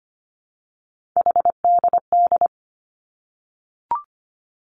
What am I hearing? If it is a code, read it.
Morse code, heard as a keyed tone, about 25 wpm, 700 Hz. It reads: HDD
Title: Three Challenging Letter-Number Combinations 25wpm